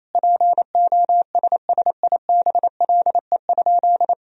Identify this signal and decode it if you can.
Morse code, heard as a keyed tone, about 28 wpm, 700 Hz. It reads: POSSIBLE?
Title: Sentences from Top 500 Words and Abbreviations 28wpm